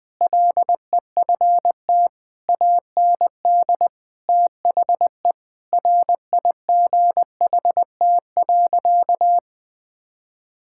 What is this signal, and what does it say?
Morse code, heard as a keyed tone, about 20 wpm, 700 Hz. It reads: LEFT AND THE RIGHT.
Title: Sentences with 7-Word Phrases - 20wpm